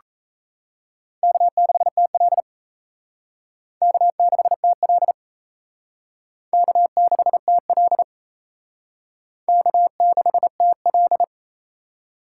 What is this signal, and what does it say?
Morse code, heard as a keyed tone, about 34 wpm, 700 Hz. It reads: K6TL K6TL K6TL K6TL
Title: Speed Racing - USA 1x2 Call Signs 28wpm